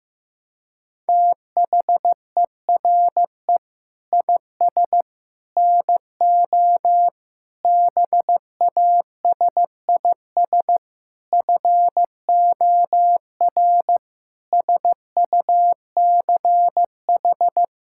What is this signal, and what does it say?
Morse code, heard as a keyed tone, about 15 wpm, 700 Hz. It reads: THERE IS NO BASIS FOR SUCH
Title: Sentences from Top 1000 Words - Review Part 5 - 15wpm